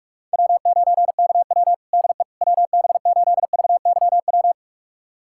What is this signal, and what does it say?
Morse code, heard as a keyed tone, about 46 wpm, 690 Hz. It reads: W9KW DE WB8VYW
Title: USA Callsign DE Callsign 45wpm